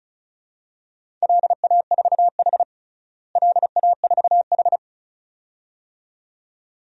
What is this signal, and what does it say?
Morse code, heard as a keyed tone, about 36 wpm, 700 Hz. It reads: LA4H LA4H
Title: DXCC-100 Call Signs - 4 Characters - Repeated 2x - 35wpm